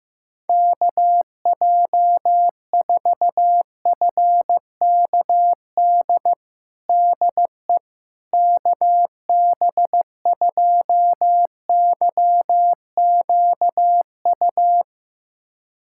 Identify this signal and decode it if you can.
Morse code, heard as a keyed tone, about 15 wpm, 700 Hz. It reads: KJ4FKD DE KB2YQU
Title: USA Callsign DE Callsign 15wpm